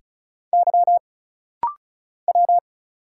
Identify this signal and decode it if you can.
Morse code, heard as a keyed tone, about 36 wpm, 700 Hz. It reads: Y W
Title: Single Letters - Rapid-Fire 35wpm